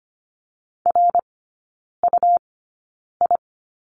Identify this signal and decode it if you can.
Morse code, heard as a keyed tone, about 25 wpm, spaced as for 9 wpm, 700 Hz. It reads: RUI
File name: course / 25:10wpm / 100 Lesson 17e - Letter u - Three Characters 25wpm